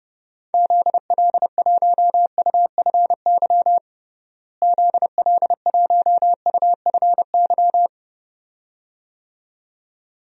Morse code, heard as a keyed tone, about 30 wpm, 700 Hz. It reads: ZL1UFY ZL1UFY
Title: DXCC-100 Call Signs - 6 Characters - Repeated 2x - 30wpm